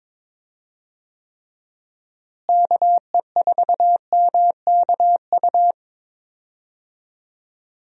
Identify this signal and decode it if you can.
Morse code, heard as a keyed tone, about 22 wpm, 700 Hz. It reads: KE4MKU
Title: USA Call Signs - 6 Characters - Repeated 3x - 22wpm